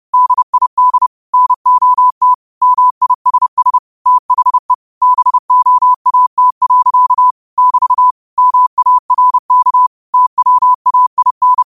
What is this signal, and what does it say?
Morse code, heard as a keyed tone, about 30 wpm, 1000 Hz. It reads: DID NOT MISS THE BOAT. <BT> MARK TWAIN
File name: qotdfiles / 30wpm